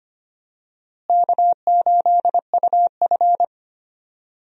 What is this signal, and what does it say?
Morse code, heard as a keyed tone, about 25 wpm, 700 Hz. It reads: K8UF
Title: USA Call Signs - 4 Characters - Repeated 2x - 25wpm